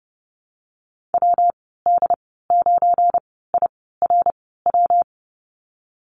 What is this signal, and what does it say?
Morse code, heard as a keyed tone, about 32 wpm, spaced as for 17 wpm, 700 Hz. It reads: WD9IRW
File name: Lesson 37g - Letter x - Call Signs 30wpm